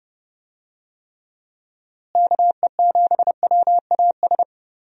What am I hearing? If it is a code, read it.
Morse code, heard as a keyed tone, about 30 wpm, 700 Hz. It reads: KE7WAS
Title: USA Call Signs - 6 Characters - Repeated 3x - 30wpm